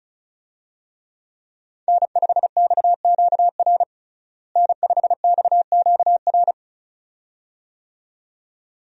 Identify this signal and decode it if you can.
Morse code, heard as a keyed tone, about 34 wpm, 700 Hz. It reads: N5XQR N5XQR
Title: USA Call Signs - 5 Characters - Repeated 2x - 35wpm